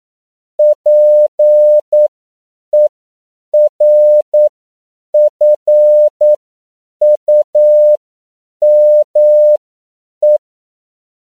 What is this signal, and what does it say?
Morse code, heard as a keyed tone, about 9 wpm, spaced as for 7 wpm, 600 Hz. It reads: PERFUME
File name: CW / 10PPM